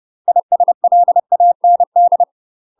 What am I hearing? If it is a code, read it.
Morse code, heard as a keyed tone, about 30 wpm, 700 Hz. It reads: ISLAND
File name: UK MASTER QSOs TRAINING / 30WPM / UK MASTER QSOs 0004